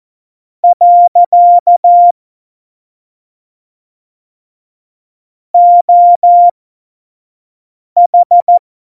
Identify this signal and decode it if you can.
Morse code, heard as a keyed tone, about 14 wpm, spaced as for 5 wpm, 700 Hz. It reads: . OH
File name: Morsekurs / Witze / Witz 24 April 2018, 25 F70